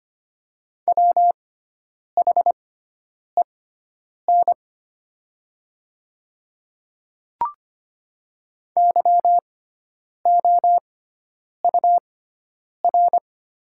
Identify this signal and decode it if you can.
Morse code, heard as a keyed tone, about 25 wpm, spaced as for 9 wpm, 700 Hz. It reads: WHEN YOUR
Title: 6x Character Spacing - Top 100 Words - 25wpm